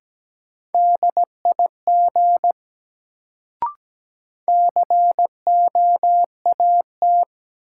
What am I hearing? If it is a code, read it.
Morse code, heard as a keyed tone, about 17 wpm, 700 Hz. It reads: DIG COAT
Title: Sentences from Top 1600 Words - New Word Drill Alt - 17wpm